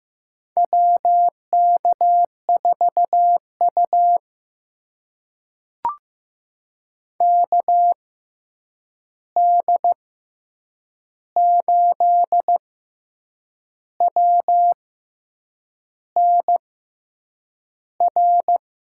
Morse code, heard as a keyed tone, about 15 wpm, spaced as for 7 wpm, 700 Hz. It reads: WK4U KD8WNR
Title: USA Call Signs - 6x Character Spacing - 15wpm